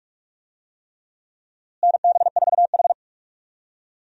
Morse code, heard as a keyed tone, about 43 wpm, 700 Hz. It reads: NB4H
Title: USA Call Signs - 4 Characters - Repeated 3x - 45wpm